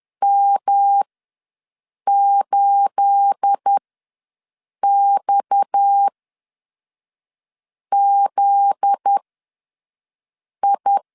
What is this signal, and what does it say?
Morse code, heard as a keyed tone, about 10 wpm, spaced as for 5 wpm, 800 Hz. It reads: M8XZI
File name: LEZIONI / 9 2867 ws 10wpm 4215